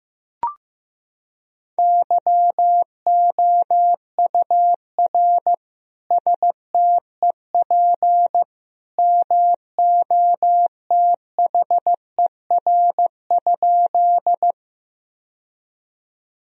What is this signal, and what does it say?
Morse code, heard as a keyed tone, about 14 wpm, 700 Hz. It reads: YOUR STEP MOTHER?
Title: Sentences from Top 1000 Words - Review Part 2 - 15wpm